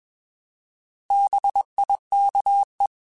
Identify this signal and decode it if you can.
Morse code, heard as a keyed tone, about 21 wpm, 800 Hz. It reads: BIKE
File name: Bs @20